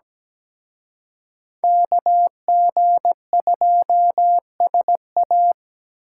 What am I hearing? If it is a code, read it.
Morse code, heard as a keyed tone, about 20 wpm, 700 Hz. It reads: KG2SA
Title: Speed Racing - USA 2x2 Call Signs 17wpm